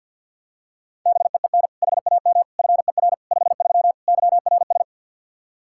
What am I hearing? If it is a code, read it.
Morse code, heard as a keyed tone, about 52 wpm, 700 Hz. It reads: BEEN HAM FER 53 YRS
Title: Practice QSO - Exchange 3 - More Info - 50wpm